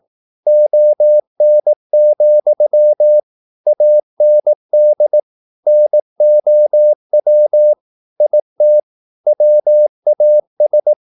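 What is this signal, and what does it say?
Morse code, heard as a keyed tone, about 18 wpm, 600 Hz. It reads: ON, AND NOW IT WAS
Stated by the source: Morse practice at 18 wpm, English text, from CW med Gnister.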